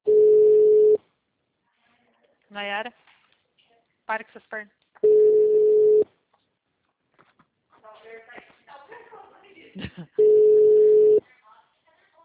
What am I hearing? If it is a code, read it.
Ringback tone (Europe, 425 Hz), with speech.